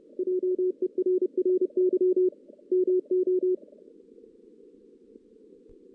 Morse code, heard as a keyed tone, about 30 wpm, 440 Hz. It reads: JERRY MO
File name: 2016 JAN CW NAQP